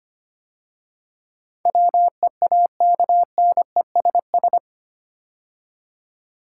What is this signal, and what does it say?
Morse code, heard as a keyed tone, about 25 wpm, 700 Hz. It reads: WEAKNESS
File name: 8 Letter Words - First 3 Letters Given - 25wpm